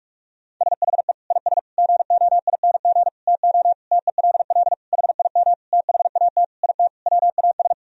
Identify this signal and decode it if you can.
Morse code, heard as a keyed tone, about 42 wpm, 700 Hz. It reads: SHE IS GOING TO TELL HIM THAT IT WAS
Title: Sentences from Top 1000 Words - Review Part 1 - 45wpm